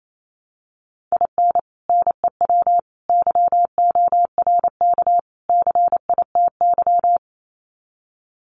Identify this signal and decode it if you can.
Morse code, heard as a keyed tone, about 28 wpm, 700 Hz. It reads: IN NEW YORK CITY
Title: Sets of 4 Words 28wpm